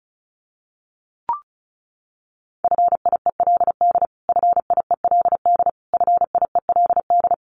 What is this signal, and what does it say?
Morse code, heard as a keyed tone, about 34 wpm, 700 Hz. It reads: FIELD FIELD FIELD
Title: Top 500 Words Repeated 35wpm